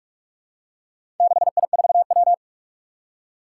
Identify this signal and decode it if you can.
Morse code, heard as a keyed tone, about 46 wpm, 700 Hz. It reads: 6I4W
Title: Four Challenging Letter-Number Combinations 45wpm